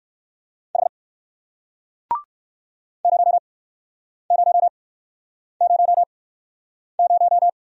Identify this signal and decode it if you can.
Morse code, heard as a keyed tone, about 54 wpm, spaced as for 9 wpm, 700 Hz. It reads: H 0000
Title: Speed-Racing Single Letter-Number 45wpm